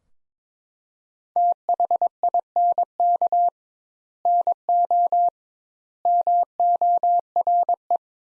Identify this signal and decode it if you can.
Morse code, heard as a keyed tone, about 22 wpm, 700 Hz. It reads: THINK NO MORE
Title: Sentences Top 200 Words - 22wpm